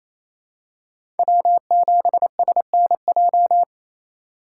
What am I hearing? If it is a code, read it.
Morse code, heard as a keyed tone, about 28 wpm, 700 Hz. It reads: W7SNJ
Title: USA Call Signs - 3x Character Spacing - 28wpm